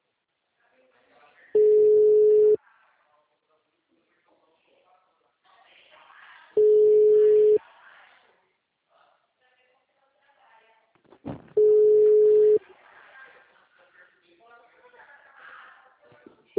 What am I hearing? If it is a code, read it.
Ringback tone (Europe, 425 Hz), with speech.